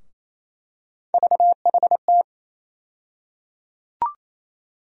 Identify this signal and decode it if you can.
Morse code, heard as a keyed tone, about 28 wpm, 700 Hz. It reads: VHT
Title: USA Call Sign Postfixes - 3 Letters 28wpm